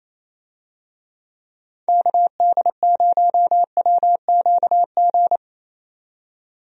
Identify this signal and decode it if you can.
Morse code, heard as a keyed tone, about 28 wpm, 700 Hz. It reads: KD0WQG
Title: USA Call Signs - 6 Characters - 28wpm